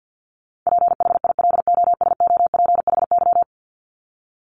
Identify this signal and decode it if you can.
Morse code, heard as a keyed tone, about 50 wpm, 700 Hz. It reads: PHILOSOPHY
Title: Top 751-1000 Words 50wpm